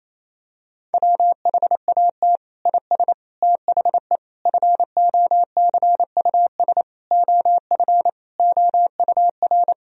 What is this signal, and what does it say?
Morse code, heard as a keyed tone, about 27 wpm, 700 Hz. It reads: WHAT IS THE FOCUS OF OUR
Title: Sentences from Top 1000 Words - Review Part 4 - 28wpm